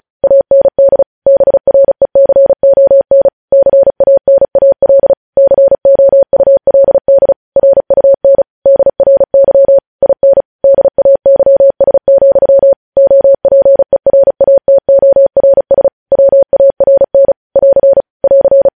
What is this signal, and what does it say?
Morse code, heard as a keyed tone, about 36 wpm, 550 Hz. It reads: AND BRECON CANAL COULD RUN DRY IN DAYS, OPERATORS WARN <AR> <AR>
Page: News headlines in Morse code at 35 wpm